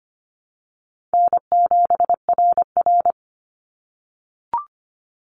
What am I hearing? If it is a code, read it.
Morse code, heard as a keyed tone, about 25 wpm, 700 Hz. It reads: N7RR
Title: NA SOTA Activator Callsigns 25wpm